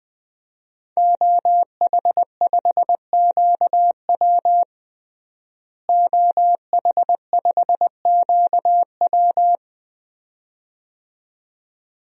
Morse code, heard as a keyed tone, about 21 wpm, 700 Hz. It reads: OH5QW OH5QW
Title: DXCC-100 Call Signs - 5 Characters - Repeated 2x - 20wpm